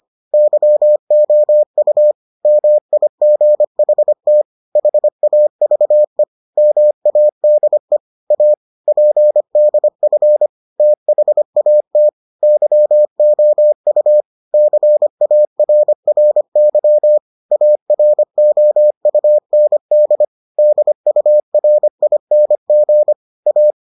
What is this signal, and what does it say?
Morse code, heard as a keyed tone, about 25 wpm, 600 Hz. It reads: YOU MIGHT HAVE MADE A PDF THAT YOU CARRY AROUND DURING A